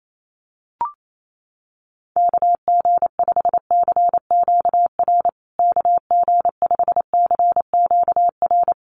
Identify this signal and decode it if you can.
Morse code, heard as a keyed tone, about 28 wpm, 700 Hz. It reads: KG5CQR KG5CQR
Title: USA Call Signs Repeated 3x 28wpm